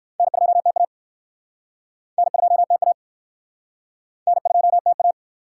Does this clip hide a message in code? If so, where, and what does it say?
Yes: Morse code from 0.2 to 5.1 s, heard as a keyed tone, about 64 wpm, 700 Hz. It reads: N2TA N2TA N2TA